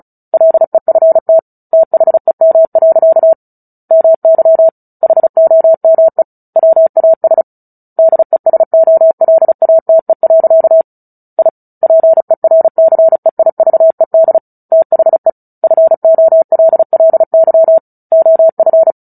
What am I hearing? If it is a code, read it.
Morse code, heard as a keyed tone, about 34 wpm, 650 Hz. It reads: LEFT THEM. MY HOME WAS DESOLATE. I PERCEIVED THE FOLLY OF
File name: Book Two / 09-Chapter 9 - 35 WPM